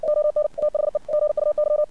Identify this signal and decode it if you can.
Morse code, heard as a keyed tone, about 56 wpm, 600 Hz. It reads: ON THE GUY
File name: archived / CRSnet